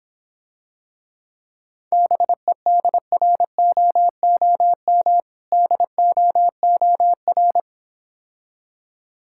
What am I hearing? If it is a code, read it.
Morse code, heard as a keyed tone, about 26 wpm, 700 Hz. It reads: BEDROOM DOOR